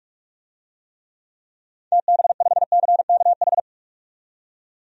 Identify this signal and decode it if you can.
Morse code, heard as a keyed tone, about 46 wpm, 700 Hz. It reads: TB5CKH
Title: DXCC-100 Call Signs - 6 Characters - 45wpm